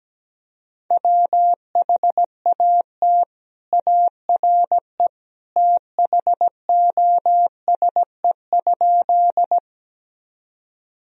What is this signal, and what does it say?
Morse code, heard as a keyed tone, about 17 wpm, 700 Hz. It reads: WHAT ARE THOSE?
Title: Sentences from Top 1000 Words - Review Part 1 - 17wpm